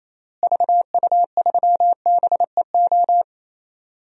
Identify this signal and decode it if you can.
Morse code, heard as a keyed tone, about 28 wpm, 700 Hz. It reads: VU3BEO
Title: DXCC-100 Call Signs - 6 Characters - Repeated 2x - 28wpm